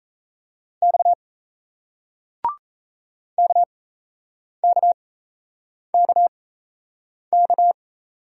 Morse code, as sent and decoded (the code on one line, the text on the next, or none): -..- / -.- -.- -.- -.-
X KKKK